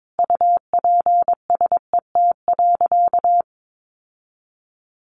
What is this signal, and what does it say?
Morse code, heard as a keyed tone, about 22 wpm, 700 Hz. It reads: UPSET.
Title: Sentences from Top 1200 Words 22wpm